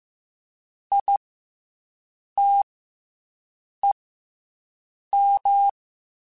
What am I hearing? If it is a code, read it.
Morse code, heard as a keyed tone, about 15 wpm, 800 Hz. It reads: I T E M
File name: Special INGs / Special INGs-mp3s / SlowToFaster ing @13